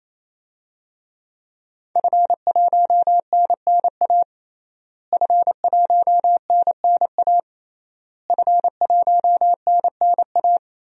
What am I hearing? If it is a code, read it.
Morse code, heard as a keyed tone, about 28 wpm, 700 Hz. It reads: F1NNA F1NNA F1NNA
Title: DXCC-100 Call Signs - 5 Characters - Repeated 3x - 28wpm